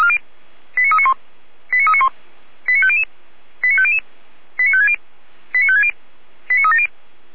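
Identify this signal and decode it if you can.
Five-tone selective call (ZVEI), heard as quick runs of five tones, tones of about 70 ms. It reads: 89391 89391 89500 89500 89560 89560 89370
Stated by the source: ZVEI five-tone selective call received by SDR.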